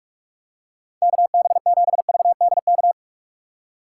Morse code, heard as a keyed tone, about 45 wpm, 700 Hz. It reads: KB7VDK